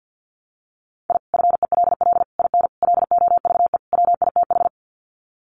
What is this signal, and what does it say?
Morse code, heard as a keyed tone, about 50 wpm, 700 Hz. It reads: I FELL IN LOVE WITH
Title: Sets of 5 Words Encore 50wpm